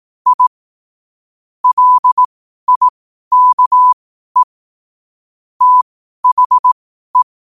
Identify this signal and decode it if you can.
Morse code, heard as a keyed tone, about 18 wpm, spaced as for 12 wpm, 1000 Hz. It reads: I LIKE THE
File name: qotdfiles / 12wpm